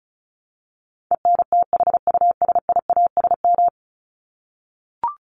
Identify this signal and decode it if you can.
Morse code, heard as a keyed tone, about 34 wpm, 700 Hz. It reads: ENTHUSIASM
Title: Top 1751-2000 Words 35wpm